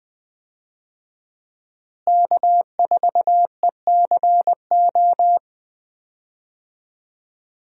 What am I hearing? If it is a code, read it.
Morse code, heard as a keyed tone, about 21 wpm, 700 Hz. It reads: K4ECO
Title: USA Call Signs - 5 Characters - Repeated 3x - 20wpm